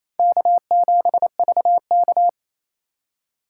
Morse code, heard as a keyed tone, about 38 wpm, 700 Hz. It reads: K7VK